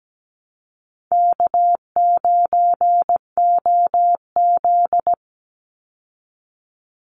Morse code, heard as a keyed tone, about 17 wpm, 700 Hz. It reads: K9OZ
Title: Easy Call Signs Repeated 3x - 17wpm